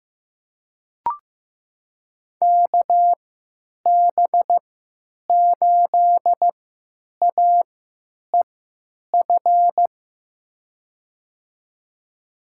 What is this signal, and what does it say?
Morse code, heard as a keyed tone, about 15 wpm, spaced as for 9 wpm, 700 Hz. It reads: KB8AEF
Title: USA Call Signs - 6 Characters - 3x Character Spacing - 15wpm